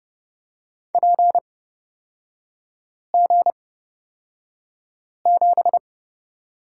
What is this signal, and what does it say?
Morse code, heard as a keyed tone, about 30 wpm, spaced as for 5 wpm, 700 Hz. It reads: PG7
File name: Lesson 36e - BT - Three Characters 30wpm